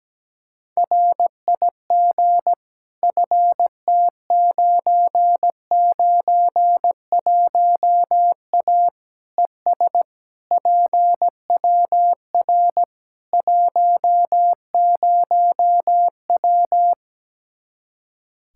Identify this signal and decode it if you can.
Morse code, heard as a keyed tone, about 17 wpm, 700 Hz. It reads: RIG FT991A ES PWR 10W
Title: Practice QSO - Exchange 2 - Info - 17wpm